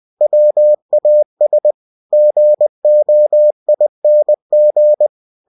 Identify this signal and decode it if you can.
Morse code, heard as a keyed tone, about 20 wpm, 600 Hz. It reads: WAS GOING
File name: cw-books / winnie-the-pooh / winnie-20wpm-page-0016